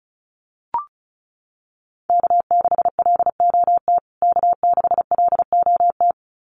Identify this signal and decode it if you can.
Morse code, heard as a keyed tone, about 34 wpm, 700 Hz. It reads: K6LOT K6LOT
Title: USA Call Signs Repeated 2x 35wpm